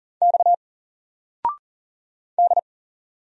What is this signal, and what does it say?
Morse code, heard as a keyed tone, about 40 wpm, 700 Hz. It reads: X D
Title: Single Character (Letter:Number:Punctuation:Prosign) 40wpm